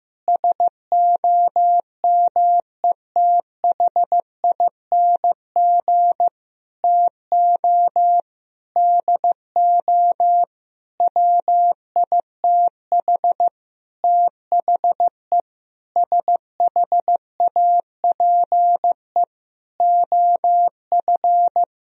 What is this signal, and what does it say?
Morse code, heard as a keyed tone, about 15 wpm, 700 Hz. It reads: SOMETHING TO DO WITH THE SHAPE OF
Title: Sentences from Top 1000 Words - Review Part 3 - 15wpm